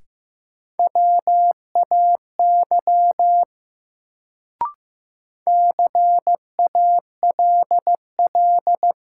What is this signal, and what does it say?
Morse code, heard as a keyed tone, about 15 wpm, 700 Hz. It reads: WAY CALL
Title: Sentences from Top 100 Words - New Word Drill - 15wpm